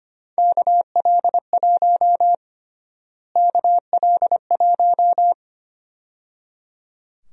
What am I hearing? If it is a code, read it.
Morse code, heard as a keyed tone, about 25 wpm, 700 Hz. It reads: KL1 KL1
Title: USA Call Sign Prefixes - Repeated 2x - 25wpm